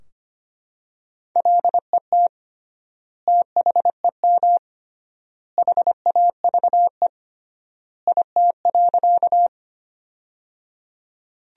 Morse code, heard as a keyed tone, about 25 wpm, 700 Hz. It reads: LET THEM HAVE IT.
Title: 3x Word-Spacing - Sentences from Top 300 Words - 25wpm